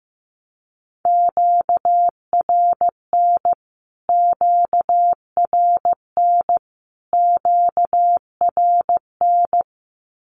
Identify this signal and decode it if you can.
Morse code, heard as a keyed tone, about 15 wpm, 700 Hz. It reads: QRN QRN QRN
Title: Q-Codes Repeated 3x 15wpm